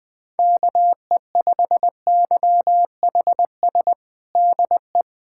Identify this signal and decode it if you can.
Morse code, heard as a keyed tone, about 20 wpm, 700 Hz. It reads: KE5YHS DE
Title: USA Callsign DE Callsign - 6 Characters - 20wpm